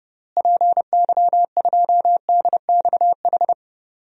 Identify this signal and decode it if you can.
Morse code, heard as a keyed tone, about 32 wpm, 700 Hz. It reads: PY2DXH
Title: DXCC-100 Call Signs - 6 Characters - Repeated 3x - 30wpm